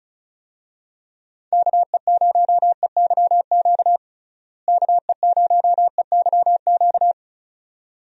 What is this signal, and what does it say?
Morse code, heard as a keyed tone, about 36 wpm, 700 Hz. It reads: KE0EYQ KE0EYQ
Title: USA Call Signs - 6 Characters - Repeated 2x - 35wpm